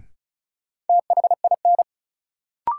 Morse code, heard as a keyed tone, about 36 wpm, 700 Hz. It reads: THIN
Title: Sentences from Top 1200 Words - New Word Drill - 35wpm